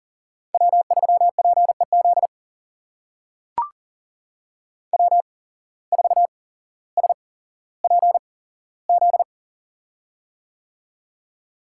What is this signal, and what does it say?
Morse code, heard as a keyed tone, about 40 wpm, spaced as for 17 wpm, 700 Hz. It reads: W3PEZ W4SPZ
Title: USA Call Signs - 5 Characters - 8x Character Spacing - 40wpm